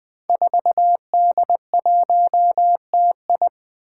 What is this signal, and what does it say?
Morse code, heard as a keyed tone, about 20 wpm, 700 Hz. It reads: 4D1TI
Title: Five Challenging Letter-Number Combinations 20wpm